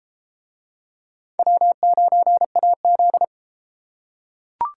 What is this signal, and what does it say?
Morse code, heard as a keyed tone, about 28 wpm, 700 Hz. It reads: W9AZ